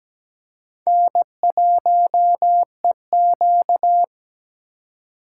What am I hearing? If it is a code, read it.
Morse code, heard as a keyed tone, about 17 wpm, 700 Hz. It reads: N1EQ
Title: USA Call Signs - 4 Characters - 3x Character Spacing - 17wpm